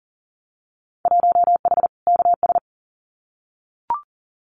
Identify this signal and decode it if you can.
Morse code, heard as a keyed tone, about 40 wpm, 700 Hz. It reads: 1H KS
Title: Winter Field Day Exchanges 40wpm